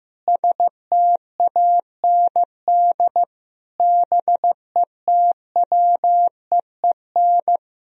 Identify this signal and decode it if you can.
Morse code, heard as a keyed tone, about 16 wpm, 700 Hz. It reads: STAND BETWEEN
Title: Sentences from Top 1000 Words - Review Part 1 - 15wpm